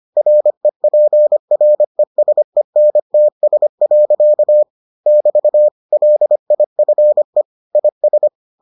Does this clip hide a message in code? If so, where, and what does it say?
Yes: Morse code from 0.2 to 8.3 s, heard as a keyed tone, about 25 wpm, 600 Hz. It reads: REPRESENTS. <BT> LIFE IS